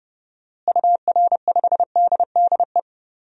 Morse code, heard as a keyed tone, about 30 wpm, 700 Hz. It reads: UR5DDE
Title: DXCC-100 Call Signs - 6 Characters - Repeated 3x - 30wpm